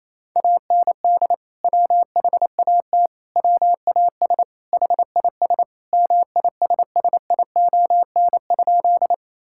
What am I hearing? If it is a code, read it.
Morse code, heard as a keyed tone, about 28 wpm, 700 Hz. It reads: AND WHAT WAS HIS MISSION?
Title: Sentences from Top 1600 Words 28wpm